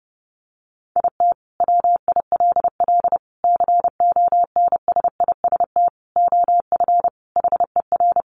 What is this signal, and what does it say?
Morse code, heard as a keyed tone, about 30 wpm, 700 Hz. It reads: IT WILL CONSIST OF HER
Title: Sentences from Top 1400 Words 30wpm